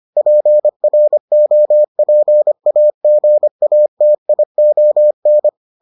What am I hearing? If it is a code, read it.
Morse code, heard as a keyed tone, about 25 wpm, 600 Hz. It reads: PROPAGATION